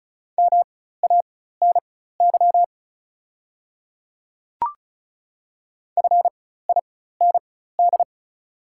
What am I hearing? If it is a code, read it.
Morse code, heard as a keyed tone, about 34 wpm, spaced as for 16 wpm, 700 Hz. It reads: MANY FIND
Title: 4x Character Spacing - Top 100 Words - 35wpm